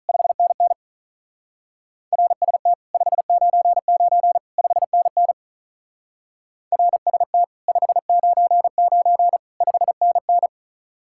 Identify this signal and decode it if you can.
Morse code, heard as a keyed tone, about 46 wpm, 700 Hz. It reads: 5NN RST 599 5NN RST 599 5NN